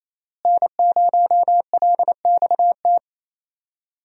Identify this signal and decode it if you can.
Morse code, heard as a keyed tone, about 28 wpm, 700 Hz. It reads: N0LXT
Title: USA Call Signs - 5 Characters - Repeated 3x - 28wpm